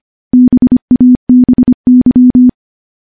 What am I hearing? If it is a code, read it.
Morse code, heard as a keyed tone, about 25 wpm, 250 Hz. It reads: BABY
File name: morse-training-25WPM